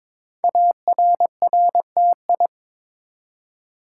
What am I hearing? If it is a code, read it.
Morse code, heard as a keyed tone, about 22 wpm, 700 Hz. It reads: ARRTI
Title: Code Groups - 5 Letters - 22wpm